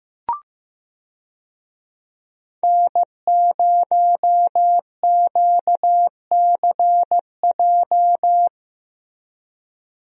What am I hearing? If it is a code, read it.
Morse code, heard as a keyed tone, about 15 wpm, 700 Hz. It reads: N0QCJ
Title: USA Call Signs - 5 Characters - 15wpm